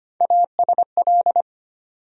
Morse code, heard as a keyed tone, about 25 wpm, 700 Hz. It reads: ASL
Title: USA Call Sign Postfixes - 3 Letters 25wpm